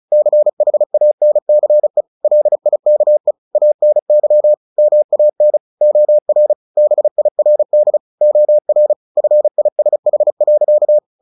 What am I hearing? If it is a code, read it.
Morse code, heard as a keyed tone, about 34 wpm, 600 Hz. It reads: CHANCE LIKE ANY MAN OR BIRD OR FISH.
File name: the old man and the sea / the old man and the sea 1 35wpm